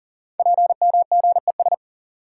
Morse code, heard as a keyed tone, about 42 wpm, 690 Hz. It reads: PMGES
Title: Code Groups - 5 Letters - 40wpm